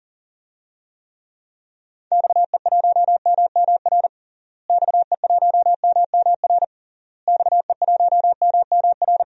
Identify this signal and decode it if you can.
Morse code, heard as a keyed tone, about 40 wpm, 700 Hz. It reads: XE1MMR XE1MMR XE1MMR
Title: DXCC-100 Call Signs - 6 Characters - Repeated 3x - 40wpm